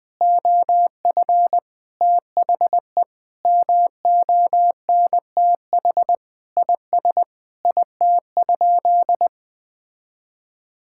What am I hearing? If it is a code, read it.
Morse code, heard as a keyed tone, about 20 wpm, 700 Hz. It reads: OF THE MONTH IS IT?
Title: Sentences from Top 1000 Words - Review Part 5 - 20wpm